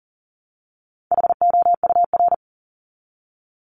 Morse code, heard as a keyed tone, about 40 wpm, 700 Hz. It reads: HOUR
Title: ES:S Suffix - Top 100 Words with Base Words 40wpm